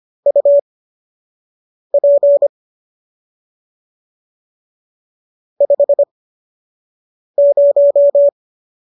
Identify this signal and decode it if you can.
Morse code, heard as a keyed tone, about 25 wpm, spaced as for 6 wpm, 570 Hz. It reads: UP 50